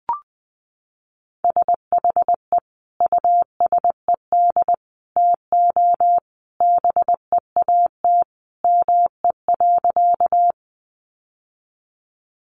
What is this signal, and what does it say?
Morse code, heard as a keyed tone, about 20 wpm, 700 Hz. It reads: SHE USED TO BEAT ME.
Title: Sentences from Top 1300 Words 20wpm